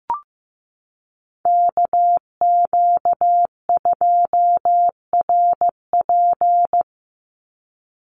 Morse code, heard as a keyed tone, about 15 wpm, 700 Hz. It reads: KQ2RP
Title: NA SOTA Chaser Callsigns 15wpm